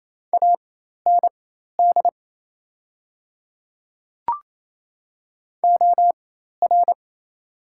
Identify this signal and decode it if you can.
Morse code, heard as a keyed tone, about 28 wpm, spaced as for 13 wpm, 700 Hz. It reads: AND OR
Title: 4x Character Spacing - Top 100 Words - 28wpm